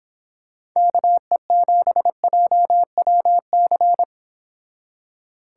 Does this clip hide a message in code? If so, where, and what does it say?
Yes: Morse code from 0.8 to 4.0 s, heard as a keyed tone, about 25 wpm, 700 Hz. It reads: KE7JWC